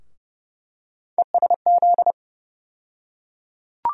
Morse code, heard as a keyed tone, about 32 wpm, 700 Hz. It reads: ESZ
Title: USA Call Sign Postfixes - 3 Letters 30wpm